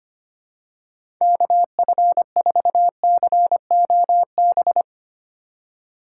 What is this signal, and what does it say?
Morse code, heard as a keyed tone, about 25 wpm, 700 Hz. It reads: KF4COB
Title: USA Call Signs - 6 Characters - 4x Character Spacing - 25wpm